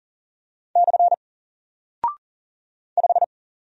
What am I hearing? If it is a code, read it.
Morse code, heard as a keyed tone, about 40 wpm, 700 Hz. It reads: / 5
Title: Single Character (Letter:Number:Punctuation:Prosign) Alt 40wpm